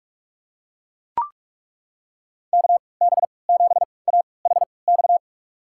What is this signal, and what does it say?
Morse code, heard as a keyed tone, about 41 wpm, spaced as for 25 wpm, 700 Hz. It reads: KB7AHX
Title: USA Call Signs - 6 Characters - 3x Character Spacing - 45wpm